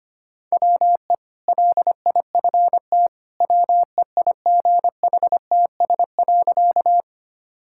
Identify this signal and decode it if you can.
Morse code, heard as a keyed tone, about 25 wpm, 690 Hz. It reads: WE LIFT WEIGHTS.
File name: Easy Sentences with Hints - 25wpm